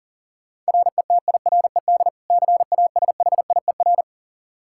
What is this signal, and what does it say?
Morse code, heard as a keyed tone, about 40 wpm, 700 Hz. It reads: RETIRED CASHIER
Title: Practice QSO - Exchange 3 - More Info - 40wpm